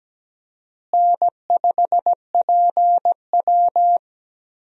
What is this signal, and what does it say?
Morse code, heard as a keyed tone, about 17 wpm, 700 Hz. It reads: N5PW
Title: USA Call Signs - 4 Characters - Repeated 2x - 17wpm